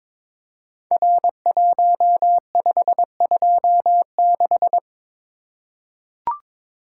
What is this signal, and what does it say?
Morse code, heard as a keyed tone, about 22 wpm, 700 Hz. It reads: R1526
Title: Code Groups - 5 Letter:Numbers - 22wpm